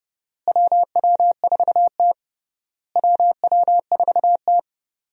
Morse code, heard as a keyed tone, about 30 wpm, 700 Hz. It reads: WW4T WW4T
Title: USA Call Signs - 4 Characters - Repeated 3x - 30wpm